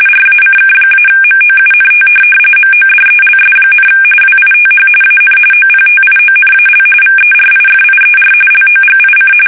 RTTY, about 45.45 baud. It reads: RACK-MOUNT MODELS, WITH AN OPTIONAL VERIFIER LOGIC
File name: fsk850 / teletype-corp